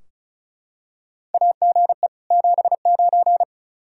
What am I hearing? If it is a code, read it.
Morse code, heard as a keyed tone, about 36 wpm, 700 Hz. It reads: AGE 79
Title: Practice QSO - Exchange 3 - More Info - 35wpm